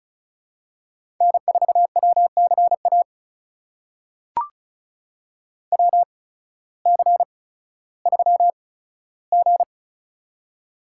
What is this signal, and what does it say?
Morse code, heard as a keyed tone, about 36 wpm, spaced as for 16 wpm, 700 Hz. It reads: N4WCA WC3G
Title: USA Call Signs - 8x Character Spacing - 35wpm